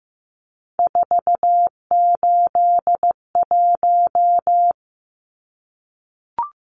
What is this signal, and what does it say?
Morse code, heard as a keyed tone, about 15 wpm, 700 Hz. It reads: 481
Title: Numbers - 3-Digit 15wpm